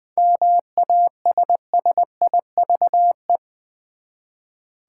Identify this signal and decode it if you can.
Morse code, heard as a keyed tone, about 20 wpm, 700 Hz. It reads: MASSIVE
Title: Seven Letter Words - 20wpm